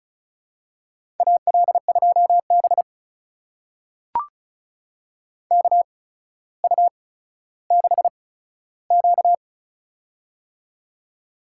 Morse code, heard as a keyed tone, about 36 wpm, spaced as for 15 wpm, 700 Hz. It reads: AL2B KU6Q
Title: USA Call Signs - 4 Characters - 8x Character Spacing - 35wpm